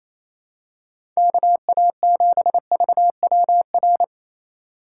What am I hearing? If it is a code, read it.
Morse code, heard as a keyed tone, about 28 wpm, 700 Hz. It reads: KA7VWR
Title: USA Call Signs - 6 Characters - 2x Character Spacing - 28wpm